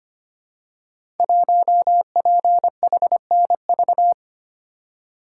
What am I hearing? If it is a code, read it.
Morse code, heard as a keyed tone, about 25 wpm, 700 Hz. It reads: 1PHNV
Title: Code Groups - 5 Letter:Numbers - 25wpm